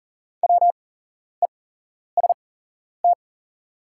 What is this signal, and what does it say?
Morse code, heard as a keyed tone, about 40 wpm, spaced as for 11 wpm, 700 Hz. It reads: WEST
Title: 8x Character Spacing - Top 500 Words - 40wpm